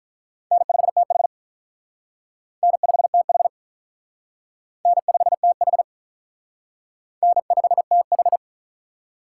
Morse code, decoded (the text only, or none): N5TH N5TH N5TH N5TH